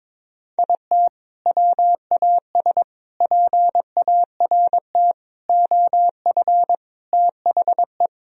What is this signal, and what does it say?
Morse code, heard as a keyed tone, about 22 wpm, 700 Hz. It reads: IT WAS PART OF THE
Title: Sentences from Top 1000 Words - Review Part 4 - 22wpm